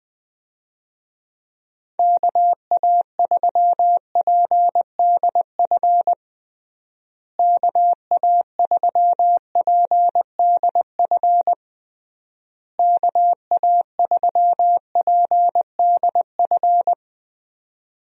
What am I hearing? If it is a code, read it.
Morse code, heard as a keyed tone, about 20 wpm, 700 Hz. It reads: KA3PDF KA3PDF KA3PDF
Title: USA Call Signs - 6 Characters - Repeated 3x - 20wpm